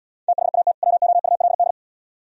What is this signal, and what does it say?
Morse code, heard as a keyed tone, about 78 wpm, 700 Hz. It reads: THAT WOULD